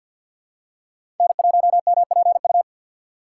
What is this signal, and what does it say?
Morse code, heard as a keyed tone, about 53 wpm, 700 Hz. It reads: N1MPU
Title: USA Call Signs - 5 Characters - 4x Character Spacing - 50wpm